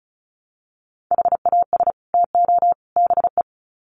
Morse code, heard as a keyed tone, about 34 wpm, 700 Hz. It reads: HAS TO BE
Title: Sets of 3 Words 35wpm